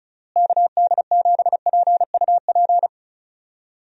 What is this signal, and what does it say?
Morse code, heard as a keyed tone, about 34 wpm, 700 Hz. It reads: KD7PUP
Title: USA Call Signs - 6 Characters - Repeated 2x - 35wpm